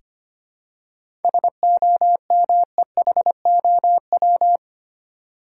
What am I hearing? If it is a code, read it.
Morse code, heard as a keyed tone, about 25 wpm, 700 Hz. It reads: SOMEHOW
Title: Seven Letter Words - 25wpm